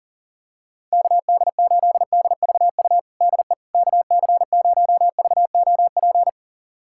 Morse code, heard as a keyed tone, about 40 wpm, 700 Hz. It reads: KD8DVU DE KC0VOP
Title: USA Callsign DE Callsign 40wpm